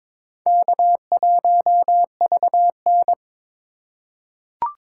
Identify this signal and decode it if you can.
Morse code, heard as a keyed tone, about 22 wpm, 700 Hz. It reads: K1VN
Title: USA Call Signs - 4 Characters - 3x Character Spacing - 22wpm